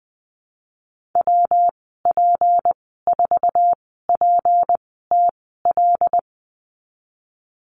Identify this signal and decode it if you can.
Morse code, heard as a keyed tone, about 19 wpm, spaced as for 14 wpm, 700 Hz. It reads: WP4PTL
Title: 186 Lesson 30g - Letter b - Call Signs 20wpm